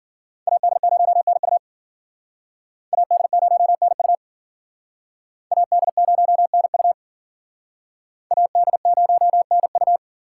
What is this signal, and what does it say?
Morse code, heard as a keyed tone, about 50 wpm, 700 Hz. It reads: AD0NU AD0NU AD0NU AD0NU
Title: Speed Racing - USA 2x2 Call Signs 40wpm